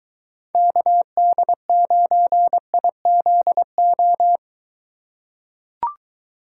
Morse code, heard as a keyed tone, about 23 wpm, 700 Hz. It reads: KD9IZO